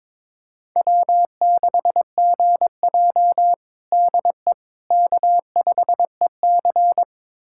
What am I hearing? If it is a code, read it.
Morse code, heard as a keyed tone, about 22 wpm, 700 Hz. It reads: W6GJ DE K5EC
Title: USA Callsign DE Callsign - 4 Characters - 22wpm